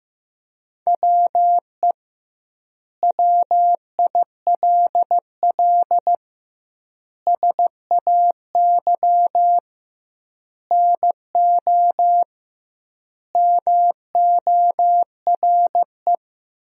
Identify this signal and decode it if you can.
Morse code, heard as a keyed tone, about 15 wpm, 700 Hz. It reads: WE WILL SAY NO MORE
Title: 2x Word-Spacing - Sentences from Top 200 Words - 15wpm